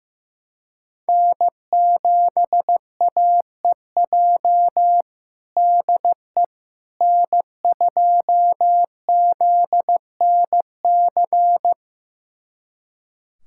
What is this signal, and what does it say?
Morse code, heard as a keyed tone, about 15 wpm, 700 Hz. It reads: N7AEJ DE N2ZNC
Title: USA Callsign DE Callsign 15wpm